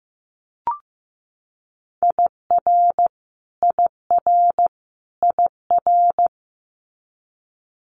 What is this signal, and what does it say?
Morse code, heard as a keyed tone, about 15 wpm, 700 Hz. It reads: IR IR IR
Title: Multi-letter Phonemes Repeated 3x 15wpm